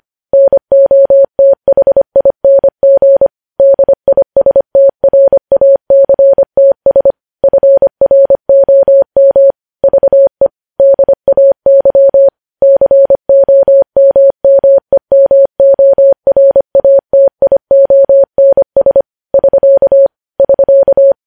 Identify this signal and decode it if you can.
Morse code, heard as a keyed tone, about 25 wpm, 550 Hz. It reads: NOTHING DISTRACTS FROM VE DAY COMMEMORATIONS <SK> <SK>
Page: News Headlines in Morse Code at 25 WPM